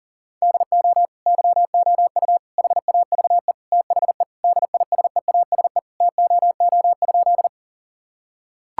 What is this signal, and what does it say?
Morse code, heard as a keyed tone, about 40 wpm, 700 Hz. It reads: DO YOU HAVE THE DISEASE TOO?
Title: Sentences from Top 1000 Words - Review Part 6 - 40wpm